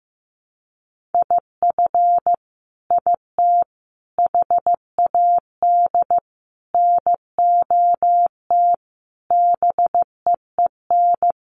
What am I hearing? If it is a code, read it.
Morse code, heard as a keyed tone, about 15 wpm, 700 Hz. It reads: IF IT HAD NOT BEEN
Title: Sets of 5 Words Encore 15wpm